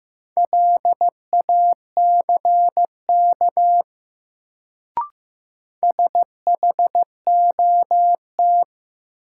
Morse code, heard as a keyed tone, about 15 wpm, 700 Hz. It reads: LACK SHOT
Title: Sentences from Top 1100 Words - New Word Drill - 15wpm